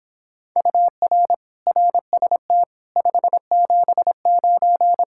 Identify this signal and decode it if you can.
Morse code, heard as a keyed tone, about 26 wpm, 700 Hz. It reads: UR RST 579